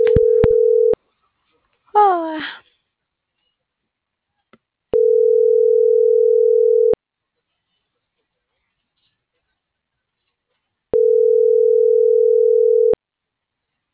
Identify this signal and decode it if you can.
Ringback tone (North America), 440 + 480 Hz.